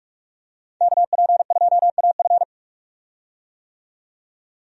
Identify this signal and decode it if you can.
Morse code, heard as a keyed tone, about 42 wpm, 700 Hz. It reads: KP2AF
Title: USA Call Signs - 5 Characters - 45wpm